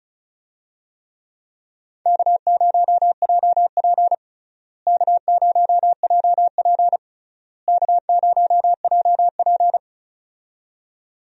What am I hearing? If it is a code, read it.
Morse code, heard as a keyed tone, about 34 wpm, 700 Hz. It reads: K0JP K0JP K0JP
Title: USA Call Signs - 4 Characters - Repeated 3x - 35wpm